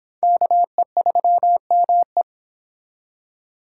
Morse code, heard as a keyed tone, about 26 wpm, 700 Hz. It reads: KE3ME